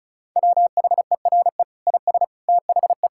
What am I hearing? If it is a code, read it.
Morse code, heard as a keyed tone, about 36 wpm, 700 Hz. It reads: WHERE IS THE
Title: Sentences from Top 1600 Words 35wpm